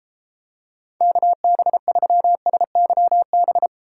Morse code, heard as a keyed tone, about 33 wpm, 700 Hz. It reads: KB3SYB